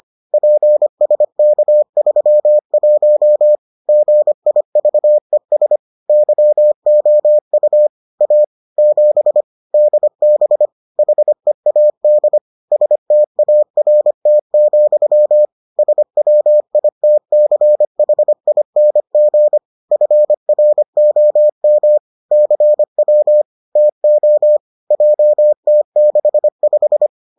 Morse code, heard as a keyed tone, about 25 wpm, 600 Hz. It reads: PSK31 GIVES YOU A 7 DB HEAD START, SWITCHING FROM CW TO JT65